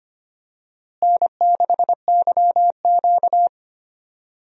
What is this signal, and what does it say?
Morse code, heard as a keyed tone, about 25 wpm, 700 Hz. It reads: N6YQ
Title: USA Call Signs - 4 Characters - Repeated 3x - 25wpm